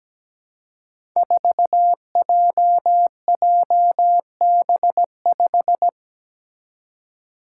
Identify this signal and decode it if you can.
Morse code, heard as a keyed tone, about 17 wpm, 700 Hz. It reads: 4JJB5
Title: Code Groups - 5 Letter:Numbers - 17wpm